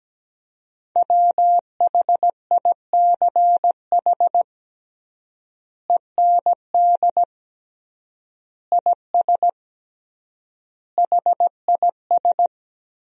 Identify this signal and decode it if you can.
Morse code, heard as a keyed tone, about 17 wpm, 700 Hz. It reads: WHICH END IS HIS
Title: 3x Word-Spacing - Sentences from Top 300 Words - 17wpm